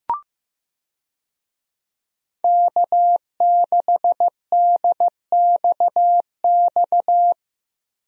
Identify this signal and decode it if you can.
Morse code, heard as a keyed tone, about 16 wpm, 700 Hz. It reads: K6DXX
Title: USA Call Signs - 5 Characters - Repeated 3x - 15wpm